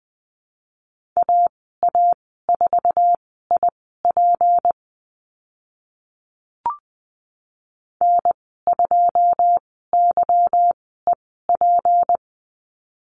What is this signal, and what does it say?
Morse code, heard as a keyed tone, about 20 wpm, spaced as for 15 wpm, 700 Hz. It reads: AA4IP N2YEP
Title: 167 Lesson 27g - Number 7 - Call Signs 20wpm